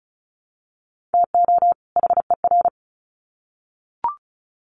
Morse code, heard as a keyed tone, about 34 wpm, 700 Hz. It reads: TO HER
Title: Sets of 2 Words 35wpm